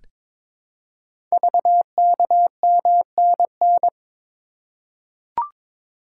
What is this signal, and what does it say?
Morse code, heard as a keyed tone, about 22 wpm, 700 Hz. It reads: VKMNN
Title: Code Groups - 5 Letters - 22wpm